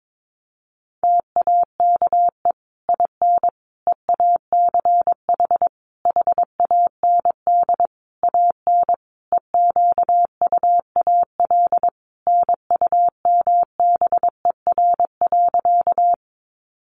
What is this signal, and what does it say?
Morse code, heard as a keyed tone, about 22 wpm, 700 Hz. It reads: TAKE IN EACH HAND AN EQUAL NUMBER.
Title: Sentences from Top 1400 Words 22wpm